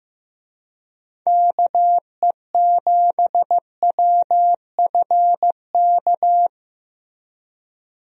Morse code, heard as a keyed tone, about 15 wpm, 700 Hz. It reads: KE7WFK
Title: USA Call Signs - 6 Characters - Repeated 2x - 15wpm